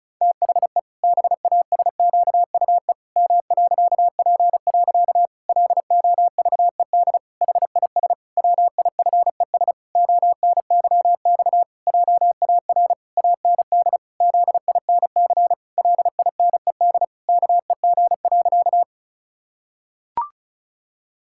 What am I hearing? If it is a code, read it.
Morse code, heard as a keyed tone, about 36 wpm, 700 Hz. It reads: THE BASQUE M.P. LOVED HIS WIFES ONYX JAR AND ZINC LINED KEG.
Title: Pangram Challenge v2 - 35wpm